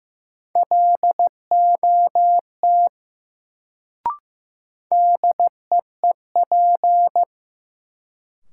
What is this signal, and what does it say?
Morse code, heard as a keyed tone, about 15 wpm, 700 Hz. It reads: LOT DEEP
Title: Sentences from Top 500 Words - New Word Drill - 15wpm